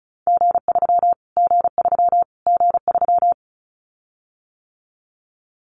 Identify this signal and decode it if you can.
Morse code, heard as a keyed tone, about 34 wpm, 700 Hz. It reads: G3 G3 G3
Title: Callsign Prefixes for Top 25 DXCC - Repeated 3x - 35wpm